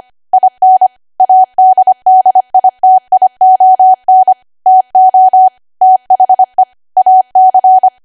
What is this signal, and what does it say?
Morse code, heard as a keyed tone, about 25 wpm, 750 Hz. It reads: IN ADDITION TO THE AC
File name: cwa / W1AW-Level3 / 25WPM